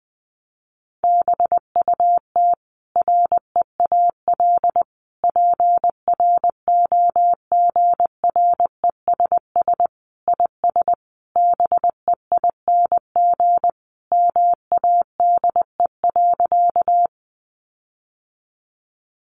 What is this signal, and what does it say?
Morse code, heard as a keyed tone, about 19 wpm, 700 Hz. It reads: BUT REAL PROGRESS IS BEING MADE.
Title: Sentences from Top 1400 Words 20wpm